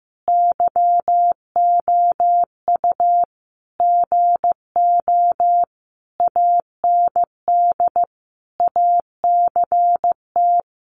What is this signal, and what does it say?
Morse code, heard as a keyed tone, about 15 wpm, 700 Hz. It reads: YOU GO AND ACT
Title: Sentences from Top 1200 Words 15wpm